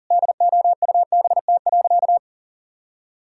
Morse code, heard as a keyed tone, about 40 wpm, 700 Hz. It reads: DOUBT.
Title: Sentences from Top 1600 Words 40wpm